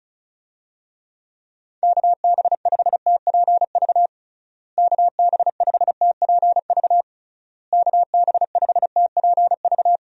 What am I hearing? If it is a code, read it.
Morse code, heard as a keyed tone, about 36 wpm, 700 Hz. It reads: KB5TPV KB5TPV KB5TPV
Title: USA Call Signs - 6 Characters - Repeated 3x - 35wpm